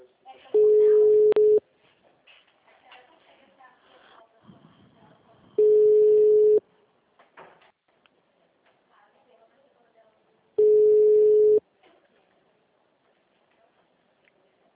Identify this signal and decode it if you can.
Ringback tone (Europe, 425 Hz), with speech.
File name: out- -2015- - - .86940